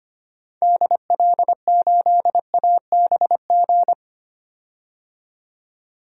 Morse code, heard as a keyed tone, about 25 wpm, 700 Hz. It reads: DL8ABG
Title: DXCC-100 Call Signs - 6 Characters - 25wpm